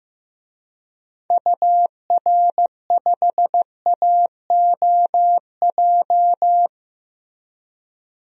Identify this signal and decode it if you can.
Morse code, heard as a keyed tone, about 15 wpm, 700 Hz. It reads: UR5AOJ
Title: DXCC-100 Call Signs - 6 Characters - Repeated 2x - 15wpm